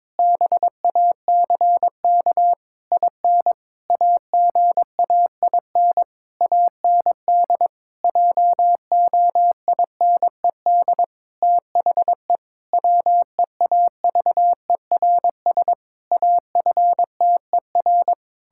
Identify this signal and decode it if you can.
Morse code, heard as a keyed tone, about 22 wpm, 700 Hz. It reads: BACK IN AGAIN AND JOINED THE WEAVERS AFTER
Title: Pangram Challenge v2 - 22wpm